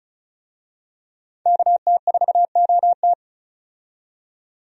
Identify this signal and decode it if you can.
Morse code, heard as a keyed tone, about 34 wpm, 700 Hz. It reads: KT4OT
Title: USA Call Signs - 5 Characters - 35wpm